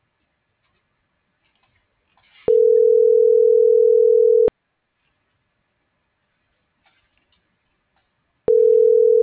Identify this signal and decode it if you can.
Ringback tone (North America), 440 + 480 Hz.